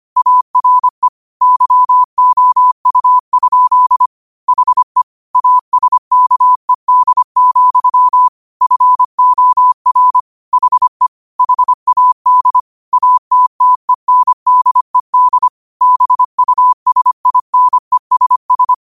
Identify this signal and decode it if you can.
Morse code, heard as a keyed tone, about 25 wpm, 1000 Hz. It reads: ARE YOU? HE ASKED, FOR HE HAD ATTENDED BUSINESS
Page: Quotes for Sat, 16 Aug 2025 in Morse Code at 25 words per minute.